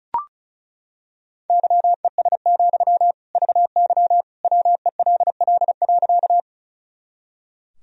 Morse code, heard as a keyed tone, about 36 wpm, 700 Hz. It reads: YES, VY WELL.
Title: Sentences from Top 500 Words and Abbreviations 35wpm